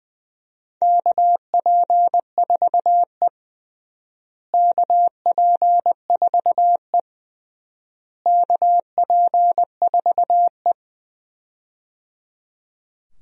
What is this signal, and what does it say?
Morse code, heard as a keyed tone, about 21 wpm, 700 Hz. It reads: KP4E KP4E KP4E
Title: DXCC-100 Call Signs - 4 Characters - Repeated 3x - 20wpm